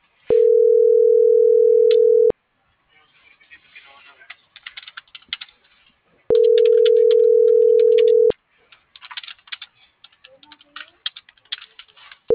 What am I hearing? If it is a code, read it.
Ringback tone (North America), 440 + 480 Hz.